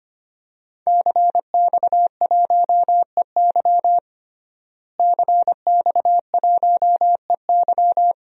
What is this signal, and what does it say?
Morse code, heard as a keyed tone, about 25 wpm, 700 Hz. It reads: CX1EY CX1EY
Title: DXCC-100 Call Signs - 5 Characters - Repeated 3x - 25wpm